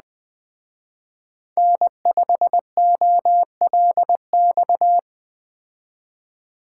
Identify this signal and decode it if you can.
Morse code, heard as a keyed tone, about 21 wpm, 700 Hz. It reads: N5OLX
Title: Speed Racing - USA 1x3 Call Signs 17wpm